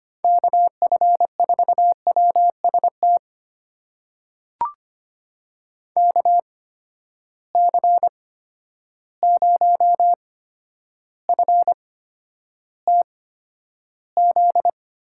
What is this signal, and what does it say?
Morse code, heard as a keyed tone, about 25 wpm, spaced as for 11 wpm, 700 Hz. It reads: KF4WST KC0FTZ
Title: USA Call Signs - 6 Characters - 8x Character Spacing - 25wpm